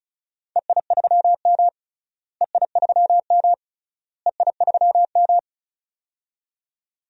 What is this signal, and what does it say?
Morse code, heard as a keyed tone, about 34 wpm, 700 Hz. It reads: EI3M EI3M EI3M
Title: DXCC-100 Call Signs - 4 Characters - Repeated 3x - 35wpm